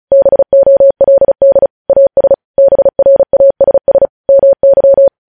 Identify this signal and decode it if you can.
Morse code, heard as a keyed tone, about 36 wpm, 550 Hz. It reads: BOLD AS BRASS MY